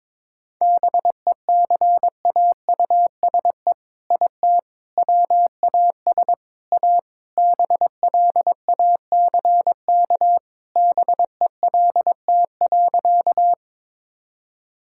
Morse code, heard as a keyed tone, about 22 wpm, 700 Hz. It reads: BECAUSE IT WAS A BLACK BELT.
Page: morse-code-ninja-practice-22wpm